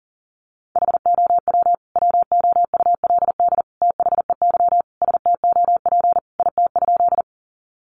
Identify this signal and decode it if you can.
Morse code, heard as a keyed tone, about 40 wpm, 700 Hz. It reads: HOW WOULD THEY STOP IT?
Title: Sentences from Top 300 Words 40wpm